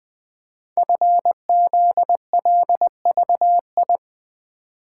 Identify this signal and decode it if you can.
Morse code, heard as a keyed tone, about 20 wpm, 700 Hz. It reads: FZLVI
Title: Code Groups - 5 Letters - 20wpm